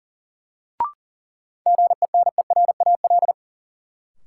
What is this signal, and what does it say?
Morse code, heard as a keyed tone, about 40 wpm, 700 Hz. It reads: GENERAL
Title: Sentences from Top 600 Words - New Word Drill - 40wpm